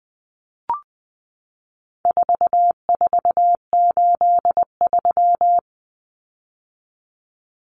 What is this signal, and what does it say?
Morse code, heard as a keyed tone, about 20 wpm, 700 Hz. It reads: 4483
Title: Common Cut Numbers - 4-Digit 20wpm